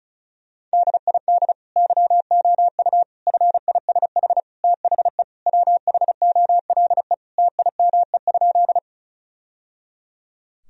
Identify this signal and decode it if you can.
Morse code, heard as a keyed tone, about 34 wpm, 700 Hz. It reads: DID YOU FISH THE WHOLE TIME?
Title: Sentences from Top 1000 Words - Review Part 1 - 35wpm